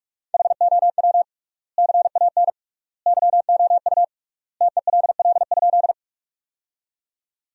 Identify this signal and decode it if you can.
Morse code, heard as a keyed tone, about 45 wpm, 700 Hz. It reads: HOW CAN YOU TELL?
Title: 3x Word-Spacing - Sentences from Top 200 Words - 45wpm